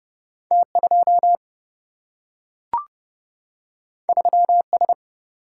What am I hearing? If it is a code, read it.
Morse code, heard as a keyed tone, about 30 wpm, 700 Hz. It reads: T2 3S
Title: Single-Letter Single-Number 30wpm